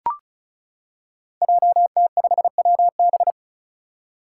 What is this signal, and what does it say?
Morse code, heard as a keyed tone, about 34 wpm, 700 Hz. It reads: JT5WB
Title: Five Challenging Letter-Number Combinations 35wpm